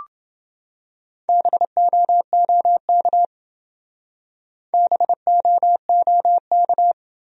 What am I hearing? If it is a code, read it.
Morse code, heard as a keyed tone, about 28 wpm, 700 Hz. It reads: BOOK BOOK